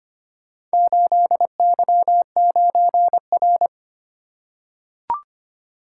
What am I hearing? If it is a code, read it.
Morse code, heard as a keyed tone, about 25 wpm, 700 Hz. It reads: 8Y9R
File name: practice / 25wpm / Four Letter-Number Combinations - 25wpm